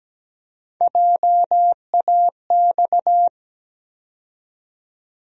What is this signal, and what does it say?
Morse code, heard as a keyed tone, about 17 wpm, 700 Hz. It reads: JAX
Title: USA Call Sign Postfixes - 3 Letters 17wpm